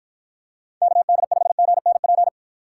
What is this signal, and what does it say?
Morse code, heard as a keyed tone, about 52 wpm, 700 Hz. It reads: KD5GNP